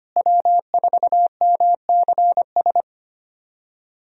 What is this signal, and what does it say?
Morse code, heard as a keyed tone, about 25 wpm, 700 Hz. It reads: W4MCS
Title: USA Call Signs - 5 Characters - Repeated 3x - 25wpm